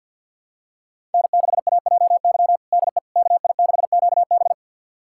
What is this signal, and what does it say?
Morse code, heard as a keyed tone, about 51 wpm, 700 Hz. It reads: N6AJY DE KI6QB
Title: USA Callsign DE Callsign - 5 Characters - 50wpm